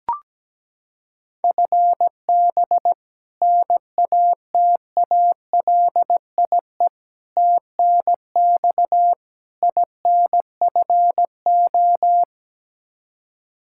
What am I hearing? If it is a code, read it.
Morse code, heard as a keyed tone, about 17 wpm, 700 Hz. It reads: FB NATALIE TNX INFO
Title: Practice QSO - Exchange 3 - More Info - 17wpm